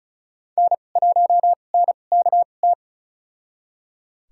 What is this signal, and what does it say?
Morse code, heard as a keyed tone, about 36 wpm, spaced as for 25 wpm, 700 Hz. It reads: N1NKT
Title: USA Call Signs - 5 Characters - 2x Character Spacing - 35wpm